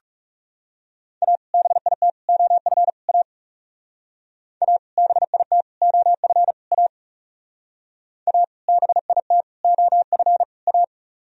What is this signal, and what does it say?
Morse code, heard as a keyed tone, about 35 wpm, 700 Hz. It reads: A BIT OF A A BIT OF A A BIT OF A
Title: Speed-Racing - Sets of 4 Words 30wpm